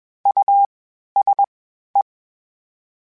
Morse code, heard as a keyed tone, about 21 wpm, spaced as for 12 wpm, 800 Hz. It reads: USE
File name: General / General-mp3s / Houses @20